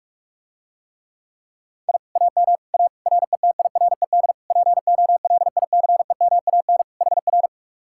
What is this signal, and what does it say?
Morse code, heard as a keyed tone, about 44 wpm, 700 Hz. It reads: I AM A RETIRED POLICEMAN HR